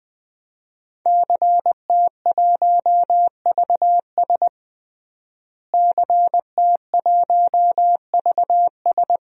Morse code, heard as a keyed tone, about 20 wpm, 700 Hz. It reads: CT1VS CT1VS
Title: DXCC-100 Call Signs - 5 Characters - Repeated 2x - 20wpm